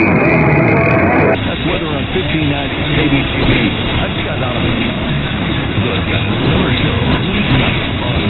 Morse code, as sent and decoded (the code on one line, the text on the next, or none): none